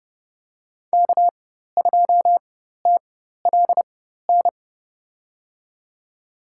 Morse code, heard as a keyed tone, about 32 wpm, spaced as for 14 wpm, 700 Hz. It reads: K2TLN
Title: USA Call Signs - 4x Character Spacing - 30wpm